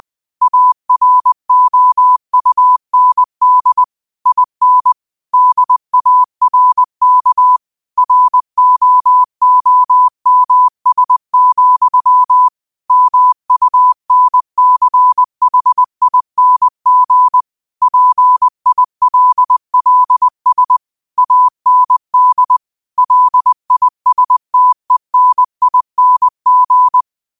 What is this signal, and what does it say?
Morse code, heard as a keyed tone, about 20 wpm, 1000 Hz. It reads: AROUND IN DARK ROOMS, MUNCHING PILLS AND LISTENING
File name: qotdfiles / 20wpm